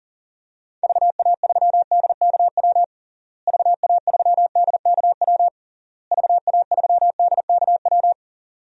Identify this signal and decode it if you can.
Morse code, heard as a keyed tone, about 37 wpm, 700 Hz. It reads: VA3DKW VA3DKW VA3DKW
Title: DXCC-100 Call Signs - 6 Characters - Repeated 3x - 40wpm